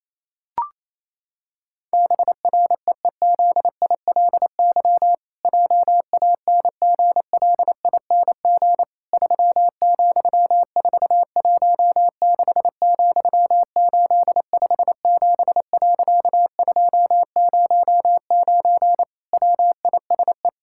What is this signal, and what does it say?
Morse code, heard as a keyed tone, about 28 wpm, 700 Hz. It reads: BREEZILY JANGLING 3,416,857.209 WISE
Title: Pangram Challenge v2 - 28wpm